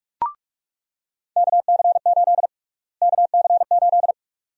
Morse code, heard as a keyed tone, about 45 wpm, 700 Hz. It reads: KC8 KC8
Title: USA Call Sign Prefixes - Repeated 2x - 45wpm